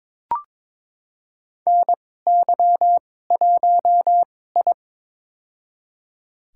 Morse code, heard as a keyed tone, about 22 wpm, spaced as for 16 wpm, 700 Hz. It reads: NY1I
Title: USA Call Signs - 4 Characters - 2x Character Spacing - 22wpm